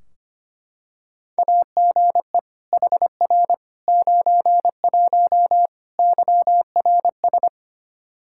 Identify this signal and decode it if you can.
Morse code, heard as a keyed tone, about 25 wpm, 700 Hz. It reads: AGE HR 91 YRS
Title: K1USN SST QSO Practice 25wpm